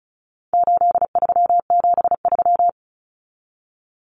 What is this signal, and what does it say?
Morse code, heard as a keyed tone, about 34 wpm, 700 Hz. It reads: 8373
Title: Numbers - 4-Digit 35wpm